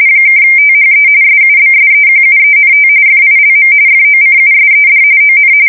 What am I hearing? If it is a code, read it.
RTTY, about 45.45 baud. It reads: (THE SUBSEQUENT FILM FOCUSED ON
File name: fsk170 / USS-New-Jersey